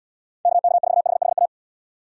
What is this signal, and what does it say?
Morse code, heard as a keyed tone, about 78 wpm, 700 Hz. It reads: KK4HHA